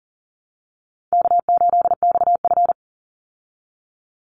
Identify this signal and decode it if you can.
Morse code, heard as a keyed tone, about 40 wpm, 700 Hz. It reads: K8XF
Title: Call Signs Repeated 3x - 40wpm